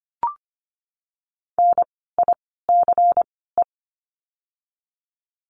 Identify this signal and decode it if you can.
Morse code, heard as a keyed tone, about 25 wpm, 700 Hz. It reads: N I C E
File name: course / 25:10wpm / 094 Lesson 16f - Letter c - Words 25wpm